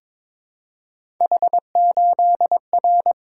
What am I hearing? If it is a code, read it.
Morse code, heard as a keyed tone, about 22 wpm, 700 Hz. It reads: H8R
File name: practice / 22wpm / Three Letter-Number Combinations - 22wpm